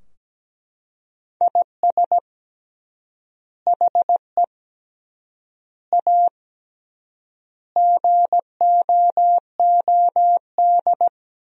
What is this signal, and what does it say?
Morse code, heard as a keyed tone, about 17 wpm, 700 Hz. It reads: IS HE A GOOD
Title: 3x Word-Spacing - Sentences from Top 200 Words - 17wpm